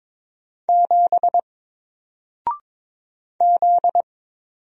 Morse code, heard as a keyed tone, about 22 wpm, 700 Hz. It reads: 7 Z
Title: Single Letter-Number 22wpm